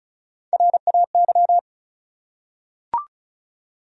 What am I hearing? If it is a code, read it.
Morse code, heard as a keyed tone, about 35 wpm, 700 Hz. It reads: RAY
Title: USA Common Names - Max 4 Letters - 35wpm